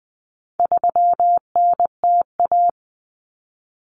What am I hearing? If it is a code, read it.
Morse code, heard as a keyed tone, about 20 wpm, 700 Hz. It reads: 3NTA
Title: Common Cut Numbers - 4-Digit 20wpm